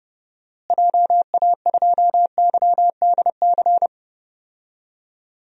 Morse code, heard as a keyed tone, about 31 wpm, 700 Hz. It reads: JA2YDC
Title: DXCC-100 Call Signs - 6 Characters - Repeated 3x - 30wpm